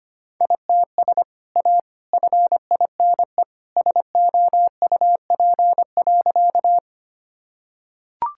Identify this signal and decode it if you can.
Morse code, heard as a keyed tone, about 25 wpm, 700 Hz. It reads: ITS A FINE SOUP.
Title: Sentences from Top 1000 Words - Review Part 6 - 25wpm